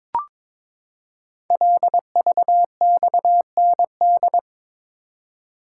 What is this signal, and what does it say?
Morse code, heard as a keyed tone, about 22 wpm, 700 Hz. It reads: LVXND
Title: Code Groups - 5 Letters - 22wpm